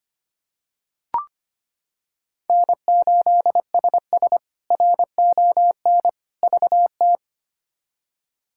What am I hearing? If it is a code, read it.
Morse code, heard as a keyed tone, about 25 wpm, 700 Hz. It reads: N8SS RON VT
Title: K1USN SST QSO Practice - 25wpm